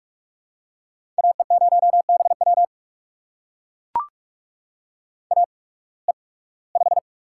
Morse code, heard as a keyed tone, about 44 wpm, spaced as for 24 wpm, 700 Hz. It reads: AE0BW AE5
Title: USA Call Signs - 5 Characters - 8x Character Spacing - 45wpm